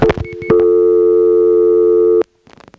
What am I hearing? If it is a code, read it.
Dial tone (North America), 350 + 440 Hz.